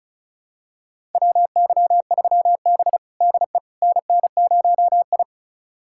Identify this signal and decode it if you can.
Morse code, heard as a keyed tone, about 36 wpm, 700 Hz. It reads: WY3B DE NN0I
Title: USA Callsign DE Callsign - 4 Characters - 35wpm